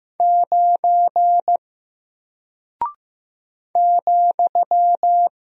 Morse code, heard as a keyed tone, about 15 wpm, 700 Hz. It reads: 9 ,
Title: Single Character (Letter:Number:Punctuation:Prosign) Alt 15wpm